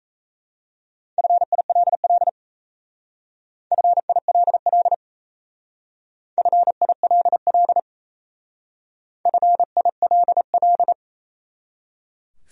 Morse code, heard as a keyed tone, about 35 wpm, 700 Hz. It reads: FILL FILL FILL FILL